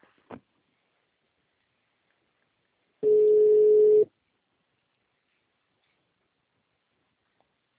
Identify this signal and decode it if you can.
Ringback tone (Europe, 425 Hz).
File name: out- -2025- - - .30996